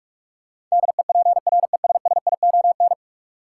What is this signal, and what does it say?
Morse code, heard as a keyed tone, about 46 wpm, 700 Hz. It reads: DEPRESSION
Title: Sentences from Top 1300 Words - New Word Drill - 45wpm